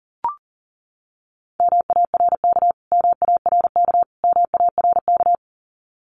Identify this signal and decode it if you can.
Morse code, heard as a keyed tone, about 40 wpm, 700 Hz. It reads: MARK MARK MARK
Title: Top 500 Words Repeated 40wpm